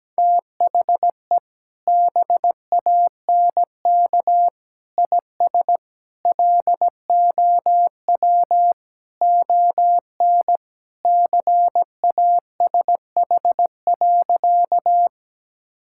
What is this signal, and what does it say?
Morse code, heard as a keyed tone, about 17 wpm, 700 Hz. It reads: THE BANK IS LOW ON CASH.
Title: Sentences from Top 1000 Words - Review Part 5 - 17wpm